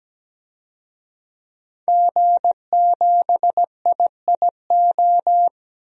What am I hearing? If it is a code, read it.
Morse code, heard as a keyed tone, about 17 wpm, 700 Hz. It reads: G7IIO
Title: DXCC-100 Call Signs - 5 Characters - 17wpm